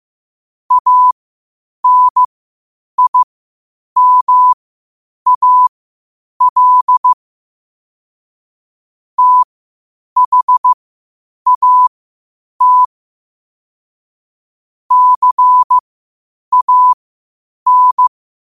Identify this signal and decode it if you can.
Morse code, heard as a keyed tone, about 15 wpm, spaced as for 8 wpm, 1000 Hz. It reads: ANIMAL THAT CAN
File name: qotdfiles / 8wpm